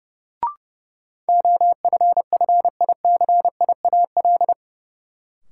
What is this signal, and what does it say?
Morse code, heard as a keyed tone, about 30 wpm, 700 Hz. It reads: OFFICIAL
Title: Sentences from Top 1000 Words - New Word Drill - 30wpm